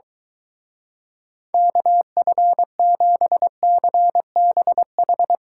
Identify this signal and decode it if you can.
Morse code, heard as a keyed tone, about 23 wpm, 700 Hz. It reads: KF7CBH